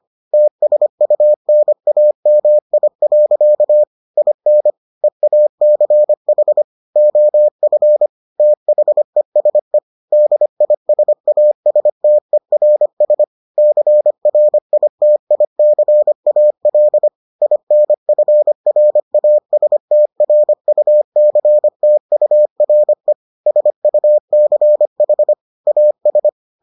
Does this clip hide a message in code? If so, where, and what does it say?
Yes: Morse code from 0.3 to 26.3 s, heard as a keyed tone, about 25 wpm, 600 Hz. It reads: TSUNAMI. IN EACH OF THESE DISASTERS CRITICAL INFRASTRUCTURE SUCH AS